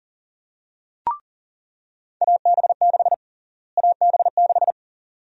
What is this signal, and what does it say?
Morse code, heard as a keyed tone, about 40 wpm, 700 Hz. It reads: AB6 AB6
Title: USA Call Sign Prefixes - Repeated 2x - 40wpm